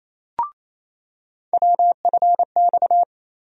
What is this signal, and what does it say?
Morse code, heard as a keyed tone, about 28 wpm, 700 Hz. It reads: WFX
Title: USA Call Sign Postfixes - 3 Letters 28wpm